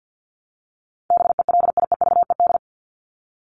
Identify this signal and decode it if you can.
Morse code, heard as a keyed tone, about 50 wpm, 700 Hz. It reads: BELIEVED
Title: ED Suffix - Top 100 Words with Base Words 50wpm